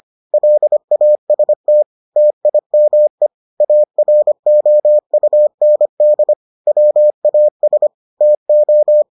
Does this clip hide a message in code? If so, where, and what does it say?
Yes: Morse code from 0.3 to 9.0 s, heard as a keyed tone, about 25 wpm, 600 Hz. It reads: LAST TIME AROUND WAS TO